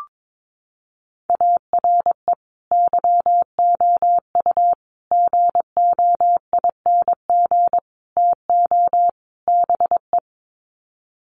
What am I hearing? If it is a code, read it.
Morse code, heard as a keyed tone, about 22 wpm, 700 Hz. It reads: ARE YOU GOING TO BE
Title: Sets of 5 Words Encore 22wpm